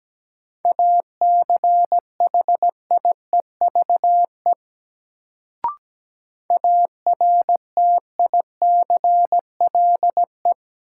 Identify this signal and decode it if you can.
Morse code, heard as a keyed tone, about 17 wpm, 700 Hz. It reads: ACHIEVE ARTICLE
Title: Sentences from Top 1000 Words - Review Part 5 - New Word Drill - 17wpm